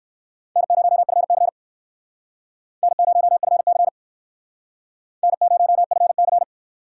Morse code, heard as a keyed tone, about 59 wpm, 700 Hz. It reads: N0FC N0FC N0FC